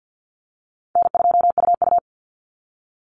Morse code, heard as a keyed tone, about 48 wpm, 700 Hz. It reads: N2UU
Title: Easy Call Signs - 50wpm